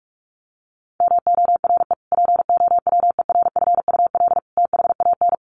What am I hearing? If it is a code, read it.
Morse code, heard as a keyed tone, about 44 wpm, 700 Hz. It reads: MORE POWERFUL THAN
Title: Sentences from Top 1300 Words 45wpm